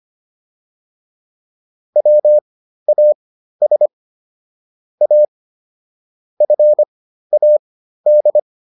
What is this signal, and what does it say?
Morse code, heard as a keyed tone, about 25 wpm, spaced as for 13 wpm, 600 Hz. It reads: WAS A FAD